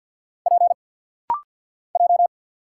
Morse code, heard as a keyed tone, about 47 wpm, 700 Hz. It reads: P J
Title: Single Letters - Rapid-Fire 50wpm